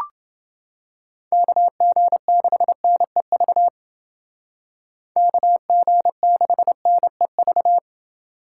Morse code, heard as a keyed tone, about 29 wpm, 700 Hz. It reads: KG6NEV KG6NEV